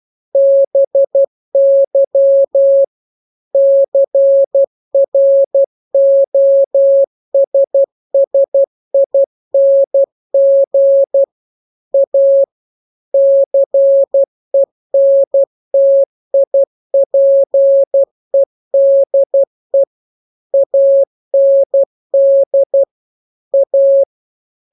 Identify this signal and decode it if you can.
Morse code, heard as a keyed tone, about 12 wpm, 550 Hz. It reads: BY CROSSING A CENTIPEDE AND A
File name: jokes / joke 12 12 0000